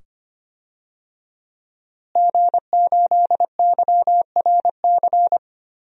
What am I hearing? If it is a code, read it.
Morse code, heard as a keyed tone, about 25 wpm, 700 Hz. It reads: G8YRC
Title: DXCC-100 Call Signs - 5 Characters - Repeated 2x - 25wpm